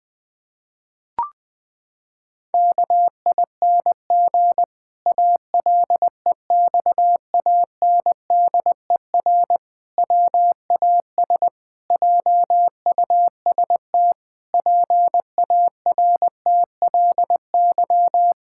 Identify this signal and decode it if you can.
Morse code, heard as a keyed tone, about 19 wpm, 700 Hz. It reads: KING ALEXANDER WAS JUST PARTLY
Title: Pangram Challenge v2 - 20wpm